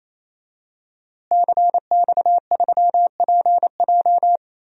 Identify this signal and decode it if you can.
Morse code, heard as a keyed tone, about 28 wpm, 700 Hz. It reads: CX3PJ
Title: DXCC-100 Call Signs - 5 Characters - Repeated 3x - 28wpm